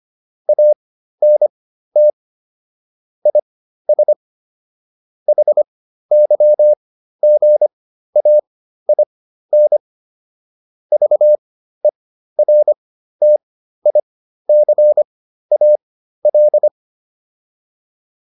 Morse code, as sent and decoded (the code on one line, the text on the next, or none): .- -. - / .. ... / .... -.-- --. .- .. -. / ...- . .-. - .. -.-. .- .-..
ANT IS HYGAIN VERTICAL